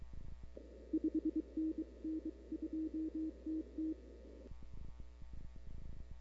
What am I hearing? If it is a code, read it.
Morse code, heard as a keyed tone, about 23 wpm, 320 Hz. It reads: 5NN 2TT
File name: CW / 2005-k5tr arrl cw dx -00:00